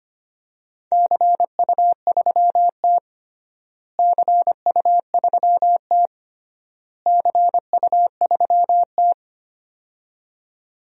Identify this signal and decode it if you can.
Morse code, heard as a keyed tone, about 25 wpm, 700 Hz. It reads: CU3T CU3T CU3T
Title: DXCC-100 Call Signs - 4 Characters - Repeated 3x - 25wpm